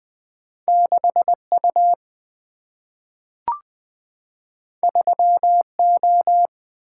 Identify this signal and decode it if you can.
Morse code, heard as a keyed tone, about 20 wpm, 690 Hz. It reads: 6U 3O
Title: Two Letter-Number Combinations - 20wpm